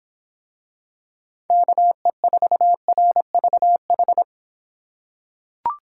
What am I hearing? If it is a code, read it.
Morse code, heard as a keyed tone, about 23 wpm, 700 Hz. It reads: KE4RVH